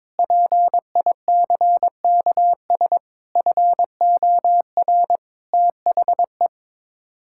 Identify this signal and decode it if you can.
Morse code, heard as a keyed tone, about 22 wpm, 700 Hz. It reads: PICKS FOR THE
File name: Sets of 9 Words - 22wpm